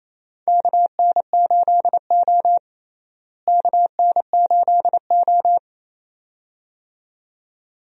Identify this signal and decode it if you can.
Morse code, heard as a keyed tone, about 28 wpm, 700 Hz. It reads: KN8O KN8O
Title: USA Call Signs - 4 Characters - Repeated 2x - 28wpm